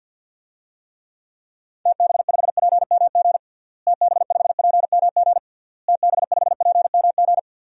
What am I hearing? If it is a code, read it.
Morse code, heard as a keyed tone, about 49 wpm, 700 Hz. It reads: TB5PMG TB5PMG TB5PMG
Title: DXCC-100 Call Signs - 6 Characters - Repeated 3x - 50wpm